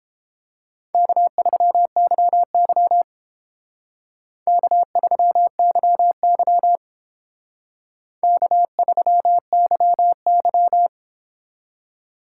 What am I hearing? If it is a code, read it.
Morse code, heard as a keyed tone, about 30 wpm, 700 Hz. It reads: K3YY K3YY K3YY